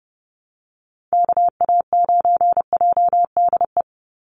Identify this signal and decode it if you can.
Morse code, heard as a keyed tone, about 30 wpm, 700 Hz. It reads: KA9JDE
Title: USA Call Signs 30wpm